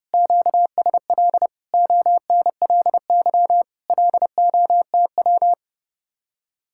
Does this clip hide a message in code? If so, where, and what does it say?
Yes: Morse code from 0.1 to 5.5 s, heard as a keyed tone, about 30 wpm, 700 Hz. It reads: QSL ONLY LOTW